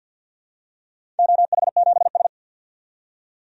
Morse code, heard as a keyed tone, about 53 wpm, 700 Hz. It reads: OH7S
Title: DXCC-100 Call Signs - 4 Characters - 50wpm